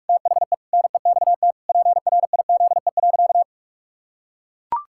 Morse code, heard as a keyed tone, about 41 wpm, 700 Hz. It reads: THE NEXT PRIZE.
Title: Pangram Challenge v2 - 45wpm